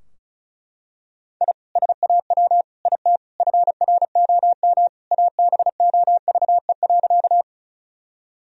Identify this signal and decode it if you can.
Morse code, heard as a keyed tone, about 36 wpm, 700 Hz. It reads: I SAW IT FROM ABOVE.
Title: Sentences from Top 1000 Words - Review Part 2 - 35wpm